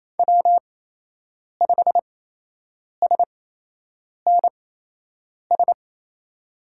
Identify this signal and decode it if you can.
Morse code, heard as a keyed tone, about 28 wpm, spaced as for 8 wpm, 700 Hz. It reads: W5SNS
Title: USA Call Signs - 8x Character Spacing - 28wpm